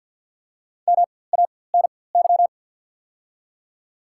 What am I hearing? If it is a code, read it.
Morse code, heard as a keyed tone, about 54 wpm, spaced as for 24 wpm, 690 Hz. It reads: MANY
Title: 4x Character Spacing - Top 300 Words - 50wpm